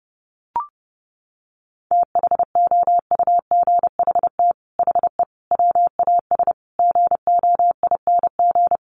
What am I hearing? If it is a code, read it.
Morse code, heard as a keyed tone, about 30 wpm, 700 Hz. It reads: THOUGHT HE WAS GOING
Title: Sets of 5 Words Encore2 30wpm